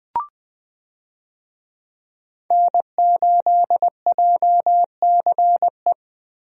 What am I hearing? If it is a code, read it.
Morse code, heard as a keyed tone, about 20 wpm, 700 Hz. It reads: N8JCE
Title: USA Call Signs - 5 Characters - Repeated 3x - 20wpm